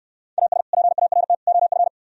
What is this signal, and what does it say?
Morse code, heard as a keyed tone, about 68 wpm, 700 Hz. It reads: AS PART OF